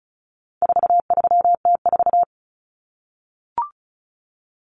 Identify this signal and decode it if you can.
Morse code, heard as a keyed tone, about 34 wpm, 700 Hz. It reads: 43T4
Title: Common Cut Numbers - 4-Digit 35wpm